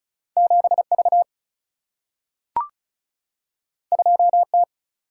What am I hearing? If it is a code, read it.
Morse code, heard as a keyed tone, about 36 wpm, 700 Hz. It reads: 7V 2T
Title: Single-Letter Single-Number 35wpm